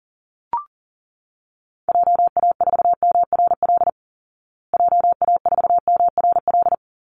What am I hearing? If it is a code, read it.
Morse code, heard as a keyed tone, about 40 wpm, 700 Hz. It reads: JA4MRL JA4MRL
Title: Call Signs Repeated 3x - 40wpm